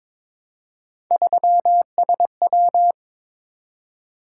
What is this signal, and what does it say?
Morse code, heard as a keyed tone, about 22 wpm, 700 Hz. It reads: 3SW
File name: practice / 22wpm / Three Letter-Number Combinations - 22wpm